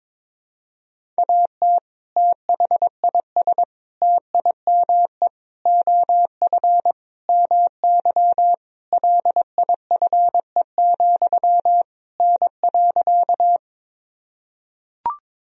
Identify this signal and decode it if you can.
Morse code, heard as a keyed tone, about 22 wpm, 700 Hz. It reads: AT THIS TIME OF MY LIFE, N.
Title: Sentences from Top 300 Words and Abbreviations 22wpm